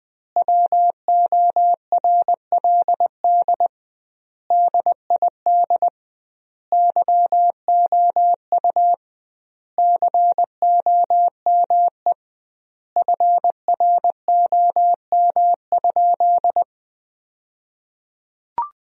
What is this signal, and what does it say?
Morse code, heard as a keyed tone, about 20 wpm, 700 Hz. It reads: WORLD DID YOU COME FROM?
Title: 2x Word-Spacing - Sentences from Top 200 Words - 20wpm